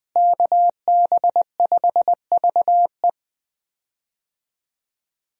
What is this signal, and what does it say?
Morse code, heard as a keyed tone, about 21 wpm, 700 Hz. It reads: KB5VE
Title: USA Call Signs - 5 Characters - Repeated 3x - 20wpm